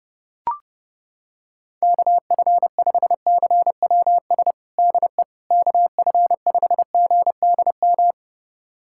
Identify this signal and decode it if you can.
Morse code, heard as a keyed tone, about 32 wpm, 700 Hz. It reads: KF5CWS DE KF5GDM
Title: USA Callsign DE Callsign - 6 Characters - 30wpm